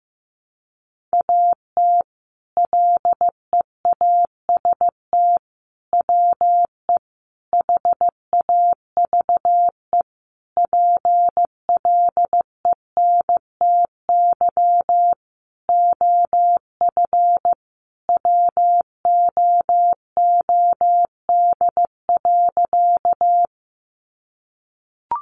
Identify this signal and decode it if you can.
Morse code, heard as a keyed tone, about 15 wpm, 700 Hz. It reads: AT LEAST WE HAVE PLENTY OF WOOD.
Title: Sentences from Top 1300 Words 15wpm